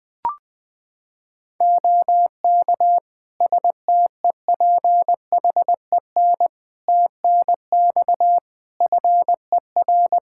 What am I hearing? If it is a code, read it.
Morse code, heard as a keyed tone, about 20 wpm, 700 Hz. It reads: OK STEPHEN TNX FER
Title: Practice QSO - Exchange 4 - Ending - 20wpm